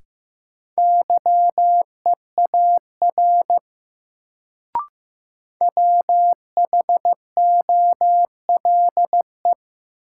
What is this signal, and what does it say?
Morse code, heard as a keyed tone, about 15 wpm, 700 Hz. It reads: YEAR WHOLE
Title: Sets of 2 Words - New Word Drill - 15wpm